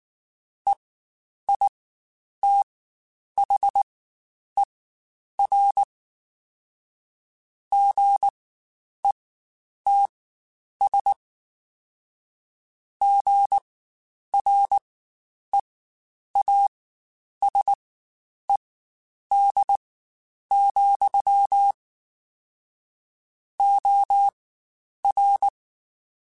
Morse code, heard as a keyed tone, about 19 wpm, spaced as for 9 wpm, 800 Hz. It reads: EITHER GETS GREASED, OR
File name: Spaced / Chuckles Spaced-mp3s / Squeaky sp @18